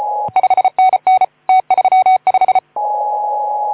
Morse code, heard as a keyed tone, about 25 wpm, 750 Hz. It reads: 5NN T35
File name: Archiv / CUC181 / OK2ZC CUC 181